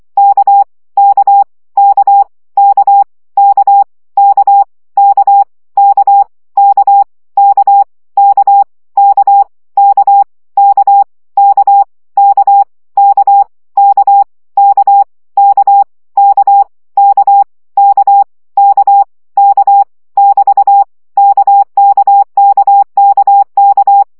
Morse code, heard as a keyed tone, about 24 wpm, 800 Hz. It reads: K K K K K K K K K K K K K K K K K K K K K K K K K <BT> KKKKK
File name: CW gyakorlas k 20 baud